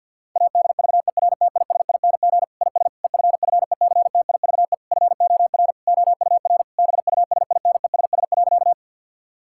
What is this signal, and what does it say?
Morse code, heard as a keyed tone, about 50 wpm, 700 Hz. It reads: ADVERTISING IS EFFECTIVE FOR OUR BUSINESS.
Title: Sentences from Top 1000 Words - Review Part 6 - 50wpm